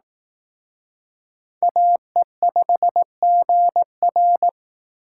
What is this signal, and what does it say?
Morse code, heard as a keyed tone, about 19 wpm, 700 Hz. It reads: AE5GR
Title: Speed Racing - USA 2x2 Call Signs 15wpm